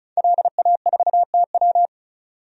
Morse code, heard as a keyed tone, about 36 wpm, 700 Hz. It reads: LA4TW
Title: DXCC-100 Call Signs - 5 Characters - 35wpm